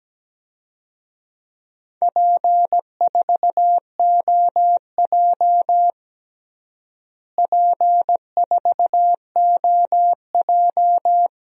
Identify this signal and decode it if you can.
Morse code, heard as a keyed tone, about 17 wpm, 700 Hz. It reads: P4OJ P4OJ
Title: DXCC-100 Call Signs - 4 Characters - Repeated 2x - 17wpm